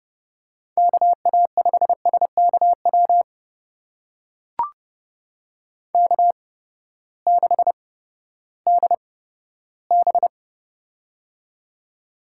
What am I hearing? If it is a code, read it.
Morse code, heard as a keyed tone, about 30 wpm, spaced as for 15 wpm, 700 Hz. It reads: KA5SKW K6DB
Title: USA Call Signs - 8x Character Spacing - 30wpm